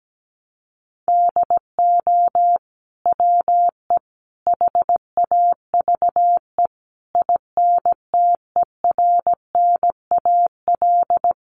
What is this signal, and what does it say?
Morse code, heard as a keyed tone, about 17 wpm, 700 Hz. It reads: DO WE HAVE INTERNAL
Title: Sentences from Top 1400 Words 17wpm